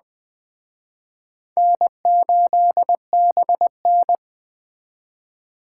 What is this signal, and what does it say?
Morse code, heard as a keyed tone, about 20 wpm, 700 Hz. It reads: N8BN